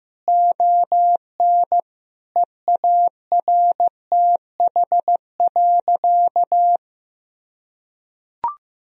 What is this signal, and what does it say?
Morse code, heard as a keyed tone, about 15 wpm, 700 Hz. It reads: ON EARTH.
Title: Sentences from Top 1000 Words - Review Part 2 - 15wpm